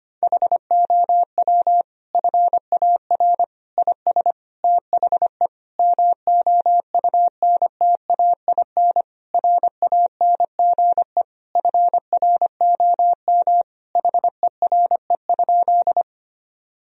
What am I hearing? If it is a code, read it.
Morse code, heard as a keyed tone, about 25 wpm, 700 Hz. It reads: HOW FAR IS THE MOUNTAIN RANGE FROM HERE?
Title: Sentences from Top 1000 Words - Review Part 4 - 25wpm